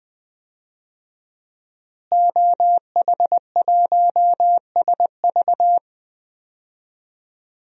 Morse code, heard as a keyed tone, about 20 wpm, 700 Hz. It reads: OH1SV
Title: DXCC-100 Call Signs - 5 Characters - Repeated 2x - 20wpm